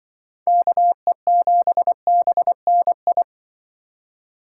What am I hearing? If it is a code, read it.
Morse code, heard as a keyed tone, about 24 wpm, 700 Hz. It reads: KE7BNI
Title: Speed Racing - USA 2x3 Call Signs 20wpm